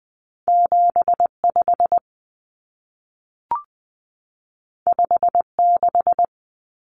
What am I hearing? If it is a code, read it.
Morse code, heard as a keyed tone, about 20 wpm, 700 Hz. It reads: 75 56
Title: Common Cut Numbers - 2-Digit 20wpm